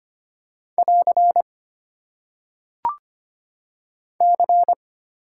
Morse code, heard as a keyed tone, about 25 wpm, 690 Hz. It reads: <AR> C
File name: Top 90 QSO Elements - 8x Character Spacing - 25wpm